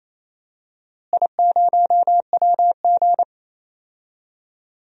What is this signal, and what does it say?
Morse code, heard as a keyed tone, about 28 wpm, 700 Hz. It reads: I0WG
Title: DXCC-100 Call Signs - 4 Characters - Repeated 2x - 28wpm